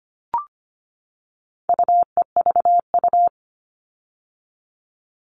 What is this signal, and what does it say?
Morse code, heard as a keyed tone, about 25 wpm, 700 Hz. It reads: UEVU
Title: All Cut Numbers - 4-Digit 25wpm